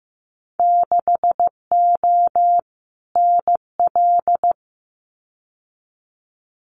Morse code, heard as a keyed tone, about 15 wpm, 700 Hz. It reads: 6O NL
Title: Winter Field Day Exchanges Encore 15wpm